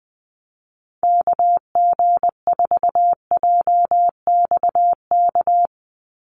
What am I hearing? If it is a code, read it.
Morse code, heard as a keyed tone, about 20 wpm, 700 Hz. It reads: KG4JXK
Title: USA Call Signs 20wpm